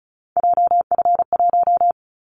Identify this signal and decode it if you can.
Morse code, heard as a keyed tone, about 36 wpm, 700 Hz. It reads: JF1
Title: Callsign Prefixes for Top 25 DXCC - 35wpm